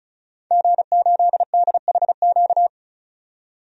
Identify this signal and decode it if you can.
Morse code, heard as a keyed tone, about 34 wpm, 700 Hz. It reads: G8DHQ
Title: DXCC-100 Call Signs - 5 Characters - 35wpm